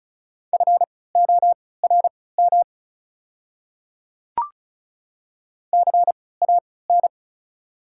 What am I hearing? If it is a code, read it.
Morse code, heard as a keyed tone, about 36 wpm, spaced as for 20 wpm, 690 Hz. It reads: FORM CAN
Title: 3x Character Spacing - Top 200 Words - 35wpm